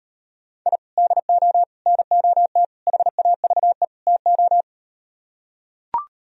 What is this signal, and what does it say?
Morse code, heard as a keyed tone, about 38 wpm, 700 Hz. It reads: I DO NOT HAVE TO